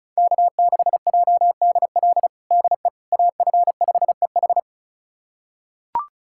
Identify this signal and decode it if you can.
Morse code, heard as a keyed tone, about 36 wpm, 690 Hz. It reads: K6JDL DE AF5EH
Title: USA Callsign DE Callsign - 5 Characters - 35wpm